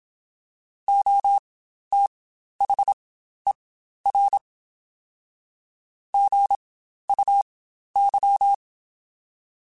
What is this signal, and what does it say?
Morse code, heard as a keyed tone, about 27 wpm, spaced as for 12 wpm, 800 Hz. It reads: OTHER GUY
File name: Spaced / Chuckles Spaced-mp3s / HumanErr sp @25